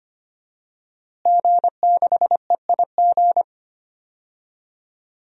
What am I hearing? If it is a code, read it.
Morse code, heard as a keyed tone, about 25 wpm, 700 Hz. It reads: G6EIG
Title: DXCC-100 Call Signs - 5 Characters - Repeated 3x - 25wpm